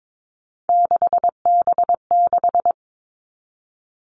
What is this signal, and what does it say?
Morse code, heard as a keyed tone, about 22 wpm, 700 Hz. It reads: 6B6
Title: All Cut Numbers - 3-Digit 22wpm